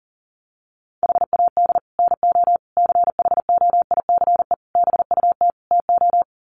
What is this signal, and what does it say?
Morse code, heard as a keyed tone, about 40 wpm, 700 Hz. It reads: HAD NO CHOICE BUT TO
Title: Sets of 5 Words 40wpm